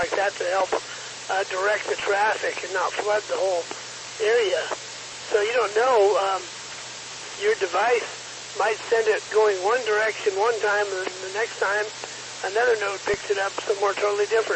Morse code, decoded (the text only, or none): none